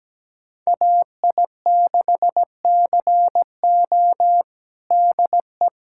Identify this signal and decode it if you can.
Morse code, heard as a keyed tone, about 17 wpm, 700 Hz. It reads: AI6CO DE
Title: USA Callsign DE Callsign 17wpm